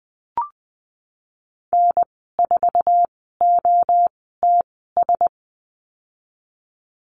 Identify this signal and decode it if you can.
Morse code, heard as a keyed tone, about 20 wpm, spaced as for 15 wpm, 700 Hz. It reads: N4OTS
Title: 186 Lesson 30g - Letter b - Call Signs 20wpm